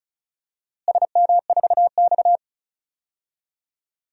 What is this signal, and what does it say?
Morse code, heard as a keyed tone, about 36 wpm, 700 Hz. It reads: SM4X
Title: DXCC-100 Call Signs - 4 Characters - Repeated 3x - 35wpm